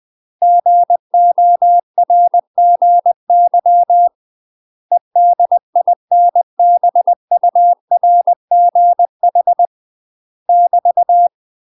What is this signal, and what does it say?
Morse code, heard as a keyed tone, about 20 wpm, 700 Hz. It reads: GORGY EDINBURGH <BT>
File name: UK MASTER QSOs TRAINING / 20WPM / UK MASTER QSOs 0017